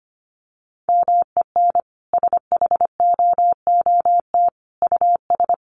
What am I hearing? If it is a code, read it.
Morse code, heard as a keyed tone, about 25 wpm, 700 Hz. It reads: MEN SHOOT US
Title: Sentences from Top 1200 Words 25wpm